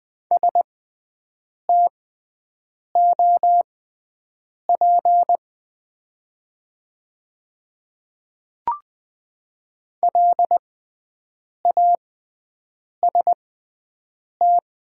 Morse code, heard as a keyed tone, about 20 wpm, spaced as for 7 wpm, 700 Hz. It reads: STOP LAST
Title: 6x Character Spacing - Top 300 Words - 20wpm